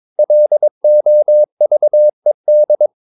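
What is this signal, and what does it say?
Morse code, heard as a keyed tone, about 22 wpm, 600 Hz. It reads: LOVED
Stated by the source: Morse code practice at 22 wpm, sending text from an English novel.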